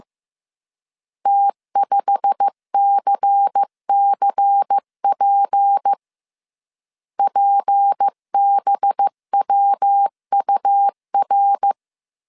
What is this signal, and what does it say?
Morse code, heard as a keyed tone, about 15 wpm, 800 Hz. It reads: T5CCP PBWUR
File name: CW / DET / 1 9 1500C S B 14wpm 3302